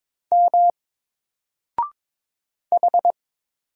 Morse code, heard as a keyed tone, about 22 wpm, 700 Hz. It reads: M H
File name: Single-Letter-Number-22wpm